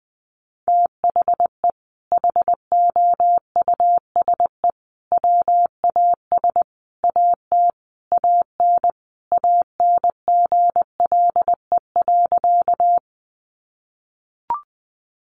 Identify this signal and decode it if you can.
Morse code, heard as a keyed tone, about 19 wpm, 700 Hz. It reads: THE HOUSE WAS AT AN ANGLE.
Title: Sentences from Top 1500 Words 20wpm